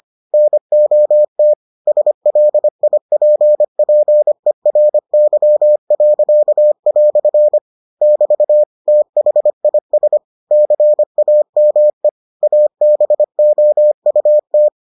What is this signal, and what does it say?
Morse code, heard as a keyed tone, about 25 wpm, 600 Hz. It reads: NOT SLIPPERY." <BT> THIS CAME ABOUT